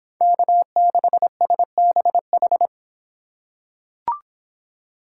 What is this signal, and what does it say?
Morse code, heard as a keyed tone, about 26 wpm, 700 Hz. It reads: K6SBH